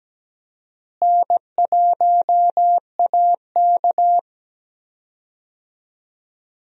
Morse code, heard as a keyed tone, about 17 wpm, 700 Hz. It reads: N1AK
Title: USA Call Signs - 4 Characters - Repeated 3x - 17wpm